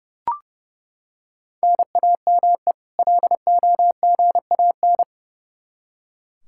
Morse code, heard as a keyed tone, about 29 wpm, 700 Hz. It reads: NAME LOGAN
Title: Practice QSO - Exchange 1 - Beginning - 30wpm